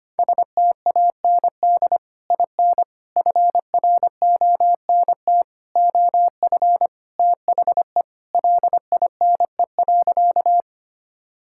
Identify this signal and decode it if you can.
Morse code, heard as a keyed tone, about 25 wpm, 700 Hz. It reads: STAND IN FRONT OF THE LINE.
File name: practice / 25wpm / Sentences with 3-Word Phrases - 25wpm